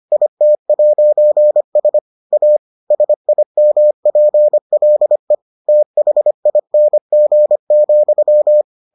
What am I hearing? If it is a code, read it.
Morse code, heard as a keyed tone, about 25 wpm, 600 Hz. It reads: IT'S A SIMPLE THING,